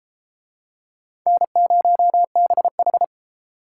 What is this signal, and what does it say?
Morse code, heard as a keyed tone, about 30 wpm, 700 Hz. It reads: N0BH